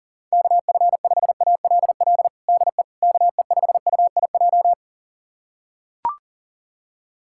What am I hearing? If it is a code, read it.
Morse code, heard as a keyed tone, about 40 wpm, 700 Hz. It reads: KF5ALL DE KE5UIJ
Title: USA Callsign DE Callsign 40wpm